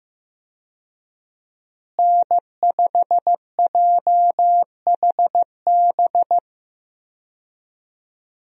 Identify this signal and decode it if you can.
Morse code, heard as a keyed tone, about 15 wpm, 700 Hz. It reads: N5JHB
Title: USA Call Signs - 5 Characters - Repeated 3x - 15wpm